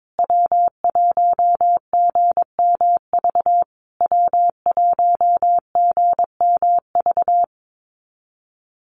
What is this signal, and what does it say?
Morse code, heard as a keyed tone, about 22 wpm, 690 Hz. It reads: W1GMV W1GMV
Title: USA Call Signs Repeated 3x 22wpm